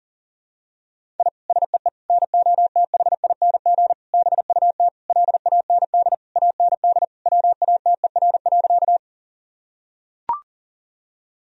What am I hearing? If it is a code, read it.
Morse code, heard as a keyed tone, about 40 wpm, 700 Hz. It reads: I SEE NOTHING BUT LAND AND WATER.
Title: Sentences from Top 1000 Words - Review Part 1 - 40wpm